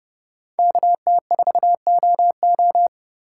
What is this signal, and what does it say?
Morse code, heard as a keyed tone, about 30 wpm, 700 Hz. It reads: KT4OO
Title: USA Call Signs - 5 Characters - 6x Character Spacing - 30wpm